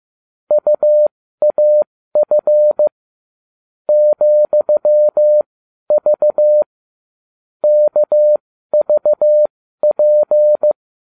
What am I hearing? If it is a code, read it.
Morse code, heard as a keyed tone, about 15 wpm, spaced as for 12 wpm, 600 Hz. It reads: UAF ,V KVP